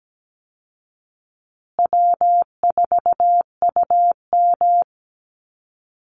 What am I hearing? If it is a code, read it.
Morse code, heard as a keyed tone, about 17 wpm, 700 Hz. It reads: W4UM
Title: Easy Call Signs Repeated 2x - 17wpm